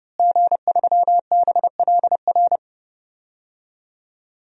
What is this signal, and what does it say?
Morse code, heard as a keyed tone, about 32 wpm, 700 Hz. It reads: G3BLR
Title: DXCC-100 Call Signs - 5 Characters - Repeated 3x - 30wpm